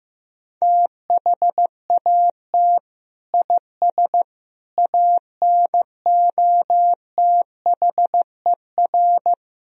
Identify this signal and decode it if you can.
Morse code, heard as a keyed tone, about 15 wpm, 700 Hz. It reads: THAT IS ANOTHER
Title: Sentences from Top 1000 Words - Review Part 1 - 15wpm